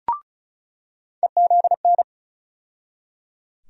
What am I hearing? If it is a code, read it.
Morse code, heard as a keyed tone, about 34 wpm, 700 Hz. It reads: EZN
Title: USA Call Sign Postfixes - 3 Letters 35wpm